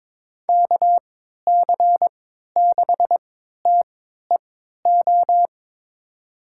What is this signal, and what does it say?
Morse code, heard as a keyed tone, about 22 wpm, spaced as for 13 wpm, 700 Hz. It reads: KC6TEO
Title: USA Call Signs - 6 Characters - 3x Character Spacing - 22wpm